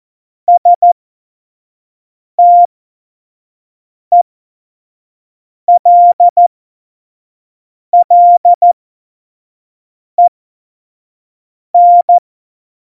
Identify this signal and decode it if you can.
Morse code, heard as a keyed tone, about 14 wpm, spaced as for 5 wpm, 700 Hz. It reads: STELLEN
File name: Morsekurs / Witze / Witz 30. Oktober 2018, 25 F70